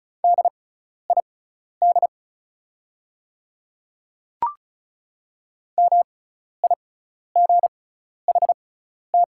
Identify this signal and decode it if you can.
Morse code, heard as a keyed tone, about 36 wpm, spaced as for 12 wpm, 690 Hz. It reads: DID MIGHT
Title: 6x Character Spacing - Top 500 Words - 35wpm